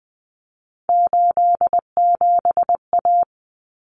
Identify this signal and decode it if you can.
Morse code, heard as a keyed tone, about 20 wpm, 700 Hz. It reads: 87A
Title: Common Cut Numbers - 3-Digit 20wpm